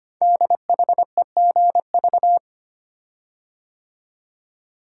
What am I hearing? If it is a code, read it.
Morse code, heard as a keyed tone, about 25 wpm, 700 Hz. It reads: DHEGV
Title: Code Groups - 5 Letters - 25wpm